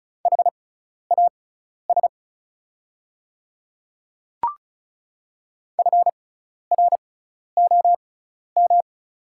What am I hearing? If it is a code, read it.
Morse code, heard as a keyed tone, about 36 wpm, spaced as for 12 wpm, 700 Hz. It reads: HAS FROM
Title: 6x Character Spacing - Top 200 Words - 35wpm